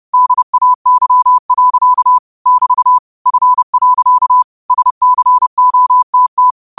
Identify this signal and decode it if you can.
Morse code, heard as a keyed tone, about 30 wpm, 1000 Hz. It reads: DAY. <BT> F. SCOTT